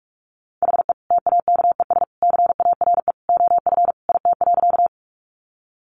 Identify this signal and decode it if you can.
Morse code, heard as a keyed tone, about 44 wpm, 700 Hz. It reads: HE TAKES CARE OF IT.
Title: Additional Easy Phrases 45wpm